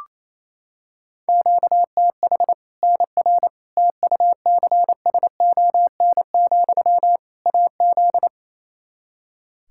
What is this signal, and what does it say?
Morse code, heard as a keyed tone, about 28 wpm, 700 Hz. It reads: QTH NR TUCSON, AZ
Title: K1USN SST QSO Practice 28wpm